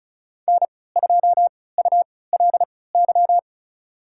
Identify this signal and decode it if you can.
Morse code, heard as a keyed tone, about 34 wpm, spaced as for 20 wpm, 700 Hz. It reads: N2ULY
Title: USA Call Signs - 5 Characters - 3x Character Spacing - 35wpm